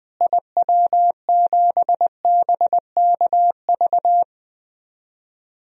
Morse code, heard as a keyed tone, about 20 wpm, 700 Hz. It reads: IW7BKV
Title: DXCC-100 Call Signs - 6 Characters - Repeated 3x - 20wpm